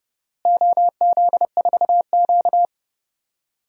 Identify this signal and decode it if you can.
Morse code, heard as a keyed tone, about 30 wpm, 700 Hz. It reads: OZ4Q
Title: DXCC-100 Call Signs - 4 Characters - Repeated 2x - 30wpm